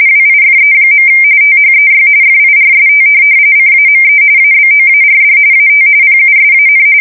RTTY, about 45.45 baud. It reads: WIFE. THEY FIRST CAME ABOARD AT PUSAN
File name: fsk170 / USS-New-Jersey